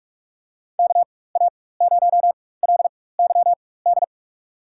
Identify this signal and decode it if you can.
Morse code, heard as a keyed tone, about 42 wpm, spaced as for 20 wpm, 700 Hz. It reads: KA0LYD
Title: USA Call Signs - 4x Character Spacing - 45wpm